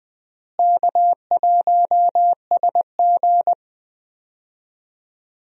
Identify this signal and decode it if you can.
Morse code, heard as a keyed tone, about 20 wpm, 700 Hz. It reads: K1SG
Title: USA Call Signs - 4 Characters - Repeated 2x - 20wpm